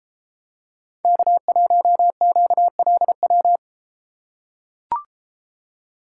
Morse code, heard as a keyed tone, about 32 wpm, 700 Hz. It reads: K1QLW